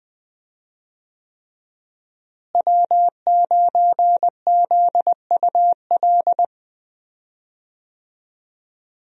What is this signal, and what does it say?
Morse code, heard as a keyed tone, about 20 wpm, 700 Hz. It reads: W9ZUL
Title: USA Call Signs - 5 Characters - Repeated 3x - 20wpm